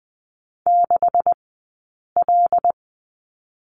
Morse code, heard as a keyed tone, about 20 wpm, spaced as for 8 wpm, 700 Hz. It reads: 6L
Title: 137 Lesson 23d - Letter f - Two Characters 20wpm